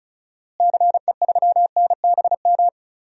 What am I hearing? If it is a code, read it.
Morse code, heard as a keyed tone, about 36 wpm, 700 Hz. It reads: CE3NBM
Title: DXCC-100 Call Signs - 6 Characters - Repeated 3x - 35wpm